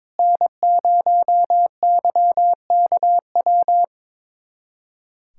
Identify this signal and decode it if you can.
Morse code, heard as a keyed tone, about 22 wpm, 700 Hz. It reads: N0YKW
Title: USA Callsign DE Callsign 22wpm